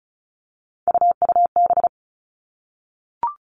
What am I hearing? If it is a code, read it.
Morse code, heard as a keyed tone, about 36 wpm, 700 Hz. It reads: UUB
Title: Three Challenging Letter-Number Combinations 35wpm